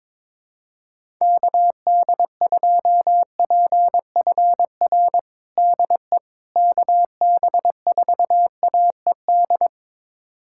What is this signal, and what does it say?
Morse code, heard as a keyed tone, about 22 wpm, 700 Hz. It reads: KD2PFR DE KB4AED
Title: USA Callsign DE Callsign - 6 Characters - 22wpm